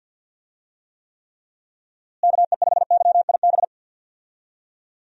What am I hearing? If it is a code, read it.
Morse code, heard as a keyed tone, about 49 wpm, 700 Hz. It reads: KE5YIB
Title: USA Call Signs - 6 Characters - Repeated 3x - 50wpm